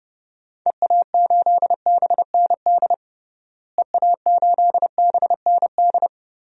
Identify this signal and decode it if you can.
Morse code, heard as a keyed tone, about 30 wpm, 700 Hz. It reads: EA8BND EA8BND
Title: DXCC-100 Call Signs - 6 Characters - Repeated 3x - 30wpm